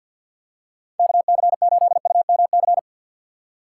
Morse code, heard as a keyed tone, about 49 wpm, 700 Hz. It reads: KC8UMC
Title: USA Call Signs - 6 Characters - 6x Character Spacing - 50wpm